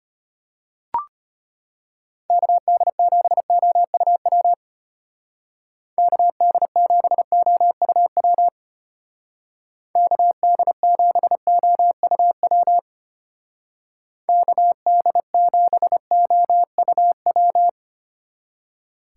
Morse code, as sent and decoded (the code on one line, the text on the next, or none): -.- -.. --... --- ..- .-- / -.- -.. --... --- ..- .-- / -.- -.. --... --- ..- .-- / -.- -.. --... --- ..- .--
KD7OUW KD7OUW KD7OUW KD7OUW